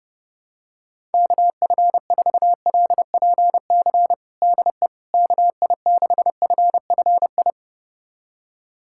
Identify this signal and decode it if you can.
Morse code, heard as a keyed tone, about 28 wpm, 700 Hz. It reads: KF4LPC DE KI6FFI
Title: USA Callsign DE Callsign - 6 Characters - 30wpm